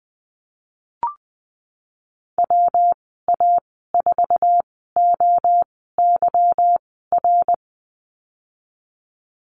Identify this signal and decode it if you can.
Morse code, heard as a keyed tone, about 20 wpm, spaced as for 15 wpm, 700 Hz. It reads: WA4OYR
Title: 179 Lesson 29g - Slash - Call Signs 20wpm